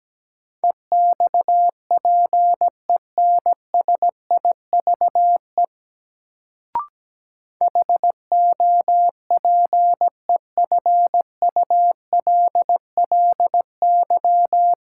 Morse code, heard as a keyed tone, about 17 wpm, 700 Hz. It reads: EXPENSIVE HOPEFULLY
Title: Sentences from Top 1500 Words - New Word Drill Alt - 17wpm